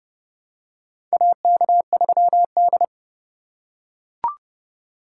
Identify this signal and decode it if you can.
Morse code, heard as a keyed tone, about 30 wpm, 700 Hz. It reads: AK3D
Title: USA Call Signs - 4 Characters - 2x Character Spacing - 30wpm